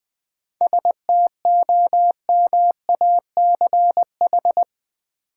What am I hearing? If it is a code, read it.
Morse code, heard as a keyed tone, about 20 wpm, 700 Hz. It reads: STOMACH
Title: Seven Letter Words - 20wpm